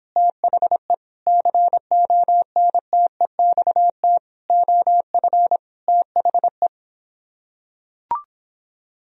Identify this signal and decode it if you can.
Morse code, heard as a keyed tone, about 26 wpm, 700 Hz. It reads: THE CONTEXT OF THE